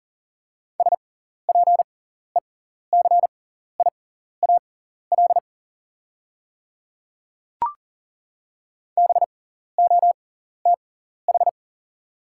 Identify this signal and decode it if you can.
Morse code, heard as a keyed tone, about 40 wpm, spaced as for 14 wpm, 690 Hz. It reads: SPECIAL BOTH
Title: 6x Character Spacing - Top 500 Words - 40wpm